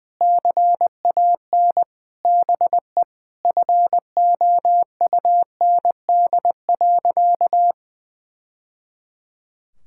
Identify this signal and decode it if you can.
Morse code, heard as a keyed tone, about 21 wpm, 700 Hz. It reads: CAN BE FOUND.
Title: Sentences from Top 1000 Words - Review Part 5 - 20wpm